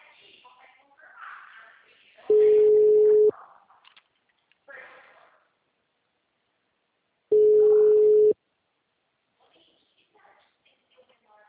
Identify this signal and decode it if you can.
Ringback tone (Europe, 425 Hz), with speech.